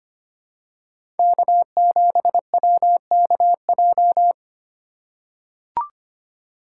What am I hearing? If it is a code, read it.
Morse code, heard as a keyed tone, about 25 wpm, 700 Hz. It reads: K7WKJ
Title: USA Call Signs - 2x Character Spacing - 25wpm